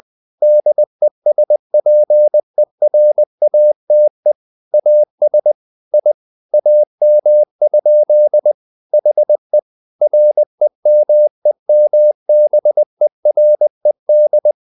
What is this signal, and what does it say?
Morse code, heard as a keyed tone, about 20 wpm, 600 Hz. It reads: DESPERATE AS I AM? HE REMEMBERED
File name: the old man and the sea / the old man and the sea part 1 20wpm